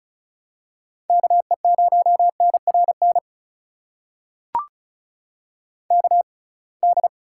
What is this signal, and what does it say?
Morse code, heard as a keyed tone, about 36 wpm, spaced as for 27 wpm, 700 Hz. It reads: KE0NRN KD
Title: USA Call Signs - 6x Character Spacing - 35wpm